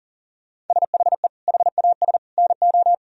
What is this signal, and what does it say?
Morse code, heard as a keyed tone, about 40 wpm, 700 Hz. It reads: SHE HAS NO
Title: Sentences from Top 1000 Words - Review Part 4 - 40wpm